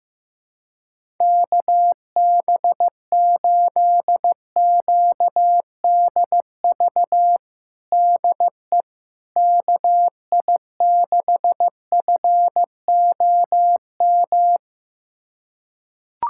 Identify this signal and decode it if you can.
Morse code, heard as a keyed tone, about 16 wpm, 700 Hz. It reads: KB8QDV DE KI6FOM
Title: USA Callsign DE Callsign - 6 Characters - 15wpm